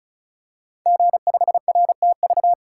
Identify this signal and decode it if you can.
Morse code, heard as a keyed tone, about 36 wpm, 690 Hz. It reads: G5RTV
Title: Code Groups - 5 Letter:Numbers - 35wpm